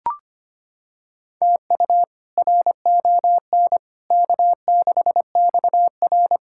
Morse code, heard as a keyed tone, about 25 wpm, 700 Hz. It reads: TU RON K6XR
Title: K1USN SST QSO Practice - 25wpm